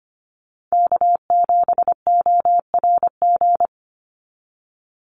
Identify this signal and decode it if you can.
Morse code, heard as a keyed tone, about 25 wpm, 700 Hz. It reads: K7ORG
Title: NA SOTA Chaser Callsigns 25wpm